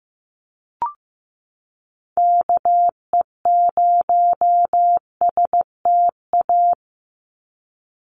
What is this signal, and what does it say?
Morse code, heard as a keyed tone, about 15 wpm, 700 Hz. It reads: KE0STA
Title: USA Call Signs 15wpm